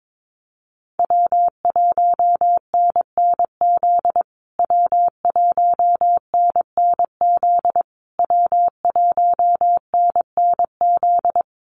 Morse code, heard as a keyed tone, about 22 wpm, 700 Hz. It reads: W1NNZ W1NNZ W1NNZ
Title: USA Call Signs Repeated 3x 22wpm